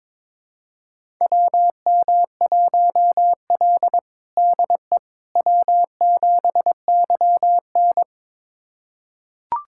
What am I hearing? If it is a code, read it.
Morse code, heard as a keyed tone, about 22 wpm, 700 Hz. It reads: WM1L DE W7YN
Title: USA Callsign DE Callsign - 4 Characters - 22wpm